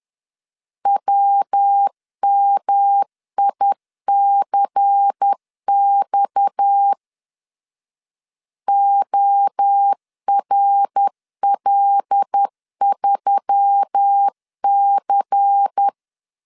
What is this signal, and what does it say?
Morse code, heard as a keyed tone, about 10 wpm, 800 Hz. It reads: WMICX ORL3C
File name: CW / DET / 1 6 1500C S B 10wpm 4137